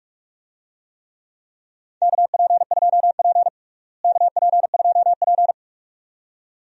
Morse code, heard as a keyed tone, about 46 wpm, 700 Hz. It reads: KP2P KP2P
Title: DXCC-100 Call Signs - 4 Characters - Repeated 2x - 45wpm